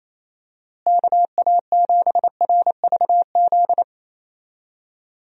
Morse code, heard as a keyed tone, about 28 wpm, 700 Hz. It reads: KA7RVZ
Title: USA Call Signs - 6 Characters - Repeated 2x - 28wpm